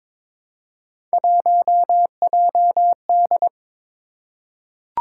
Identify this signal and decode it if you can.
Morse code, heard as a keyed tone, about 23 wpm, 700 Hz. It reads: 1JD
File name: practice / 22wpm / Three Letter-Number Combinations - 22wpm